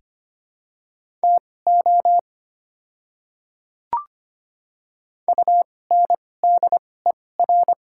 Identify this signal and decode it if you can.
Morse code, heard as a keyed tone, about 25 wpm, spaced as for 18 wpm, 700 Hz. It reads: TO UNDER
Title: 2x Character Spacing - Top 300 Words - 25wpm